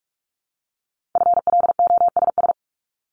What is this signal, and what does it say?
Morse code, heard as a keyed tone, about 46 wpm, 700 Hz. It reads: FLOSS
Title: Multi-letter Phonemes 45wpm